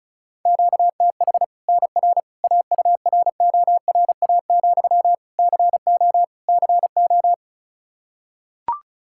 Morse code, heard as a keyed tone, about 36 wpm, 700 Hz. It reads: QTH NR AURORA, CO CO
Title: Practice QSO - Exchange 1 - Beginning - 35wpm